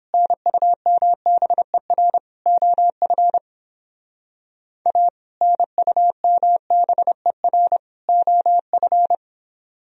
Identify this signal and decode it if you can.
Morse code, heard as a keyed tone, about 29 wpm, 700 Hz. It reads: NUMBER OF A NUMBER OF